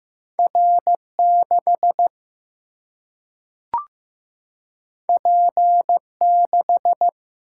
Morse code, heard as a keyed tone, about 15 wpm, 700 Hz. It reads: R 6 P 6
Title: Single-Letter Single-Number 15wpm